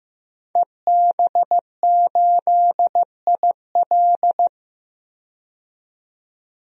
Morse code, heard as a keyed tone, about 15 wpm, 700 Hz. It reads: EB8IL
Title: DXCC-100 Call Signs - 5 Characters - 15wpm